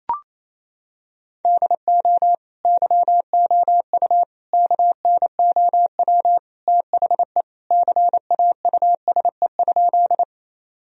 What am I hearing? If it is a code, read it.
Morse code, heard as a keyed tone, about 28 wpm, 700 Hz. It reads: DO YOU KNOW THE CAUSE?
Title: Sentences from Top 1000 Words - Review Part 1 - 28wpm